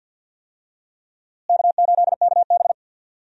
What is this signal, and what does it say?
Morse code, heard as a keyed tone, about 46 wpm, 700 Hz. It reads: K8KB
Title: USA Call Signs - 4 Characters - Repeated 3x - 50wpm